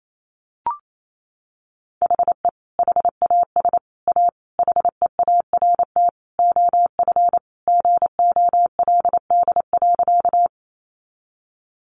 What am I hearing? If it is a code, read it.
Morse code, heard as a keyed tone, about 28 wpm, 700 Hz. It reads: HE HAS A HEART OF GOLD.
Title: Sentences from Top 1200 Words 28wpm